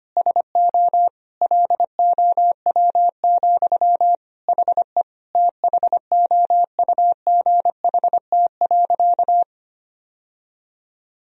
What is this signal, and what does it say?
Morse code, heard as a keyed tone, about 25 wpm, 700 Hz. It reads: SO LOW, HE THOUGHT.
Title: Sentences from Top 1000 Words - Review Part 1 - 25wpm